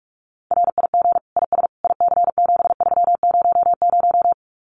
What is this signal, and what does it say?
Morse code, heard as a keyed tone, about 44 wpm, 700 Hz. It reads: RIG IS IC7300
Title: Common QSO Phrases Encore 45wpm